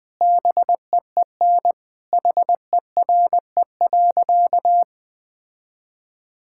Morse code, heard as a keyed tone, about 20 wpm, 700 Hz. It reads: BEEN HERE.
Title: Sentences with 4-Word Phrases - 20wpm